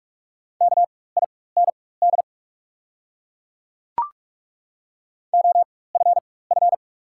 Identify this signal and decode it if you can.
Morse code, heard as a keyed tone, about 43 wpm, spaced as for 21 wpm, 700 Hz. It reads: KIND OFF
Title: 4x Character Spacing - Top 300 Words - 45wpm